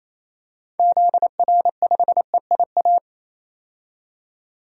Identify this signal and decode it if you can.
Morse code, heard as a keyed tone, about 28 wpm, 700 Hz. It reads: ZR5EIA
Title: DXCC-100 Call Signs - 6 Characters - 28wpm